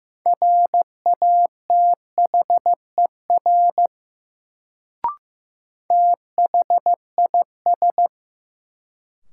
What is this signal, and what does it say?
Morse code, heard as a keyed tone, about 15 wpm, 700 Hz. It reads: RATHER THIS
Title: Sets of 2 Words - New Word Drill - 15wpm